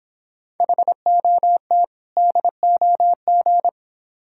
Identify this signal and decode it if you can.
Morse code, heard as a keyed tone, about 26 wpm, 700 Hz. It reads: HOT DOG